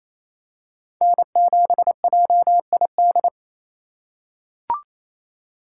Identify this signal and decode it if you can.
Morse code, heard as a keyed tone, about 28 wpm, 700 Hz. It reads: N7JID
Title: USA Call Signs - 5 Characters - 4x Character Spacing - 28wpm